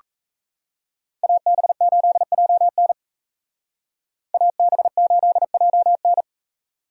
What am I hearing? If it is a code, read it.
Morse code, heard as a keyed tone, about 33 wpm, 700 Hz. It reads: AB8JN AB8JN
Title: Speed Racing - USA 2x2 Call Signs 28wpm